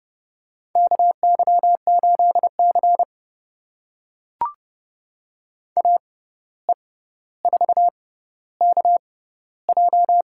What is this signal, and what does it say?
Morse code, heard as a keyed tone, about 30 wpm, spaced as for 14 wpm, 700 Hz. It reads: KY8C AE4KJ
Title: USA Call Signs - 6x Character Spacing - 30wpm